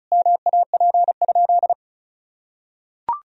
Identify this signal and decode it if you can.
Morse code, heard as a keyed tone, about 36 wpm, 700 Hz. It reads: MAP?
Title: Sentences from Top 1000 Words - Review Part 2 - 35wpm